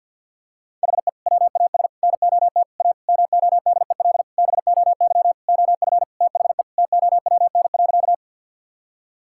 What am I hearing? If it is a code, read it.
Morse code, heard as a keyed tone, about 50 wpm, 700 Hz. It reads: HE WAS NOT A MODEL BOY OF THE TOWN.
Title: Sentences from Top 1000 Words - Review Part 5 - 50wpm